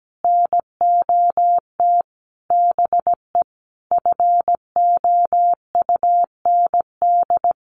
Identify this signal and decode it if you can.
Morse code, heard as a keyed tone, about 17 wpm, 700 Hz. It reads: NOT BE FOUND
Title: Sentences from Top 1400 Words 17wpm